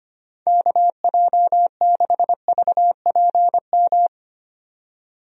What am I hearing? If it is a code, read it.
Morse code, heard as a keyed tone, about 25 wpm, 700 Hz. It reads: KJ6VPM
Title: USA Call Signs - 6 Characters - 25wpm